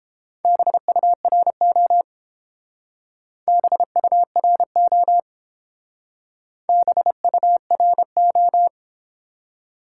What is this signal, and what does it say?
Morse code, heard as a keyed tone, about 29 wpm, 700 Hz. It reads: BURO BURO BURO